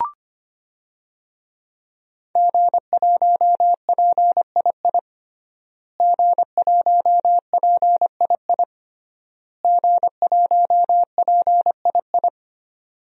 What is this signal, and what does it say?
Morse code, heard as a keyed tone, about 25 wpm, 700 Hz. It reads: G1PII G1PII G1PII
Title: DXCC-100 Call Signs - 5 Characters - Repeated 3x - 25wpm